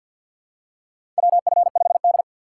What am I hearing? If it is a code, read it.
Morse code, heard as a keyed tone, about 47 wpm, 700 Hz. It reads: WF5D
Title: USA Call Signs - 4 Characters - 6x Character Spacing - 50wpm